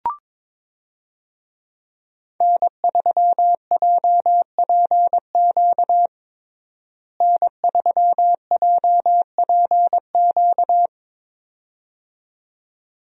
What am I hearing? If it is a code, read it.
Morse code, heard as a keyed tone, about 22 wpm, 700 Hz. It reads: N3JPQ N3JPQ
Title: USA Call Signs - 5 Characters - Repeated 2x - 22wpm